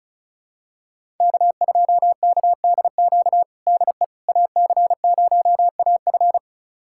Morse code, heard as a keyed tone, about 36 wpm, 700 Hz. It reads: K2KDQ DE AC0AF
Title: USA Callsign DE Callsign - 5 Characters - 35wpm